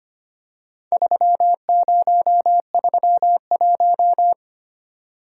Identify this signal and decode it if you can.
Morse code, heard as a keyed tone, about 25 wpm, 700 Hz. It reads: 3031
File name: practice / 25wpm / Four Letter-Number Combinations - 25wpm